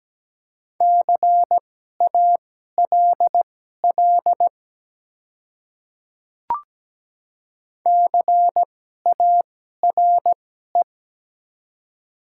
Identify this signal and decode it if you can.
Morse code, heard as a keyed tone, about 17 wpm, spaced as for 12 wpm, 700 Hz. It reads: CALL CARE
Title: 2x Character Spacing - Top 300 Words - 17wpm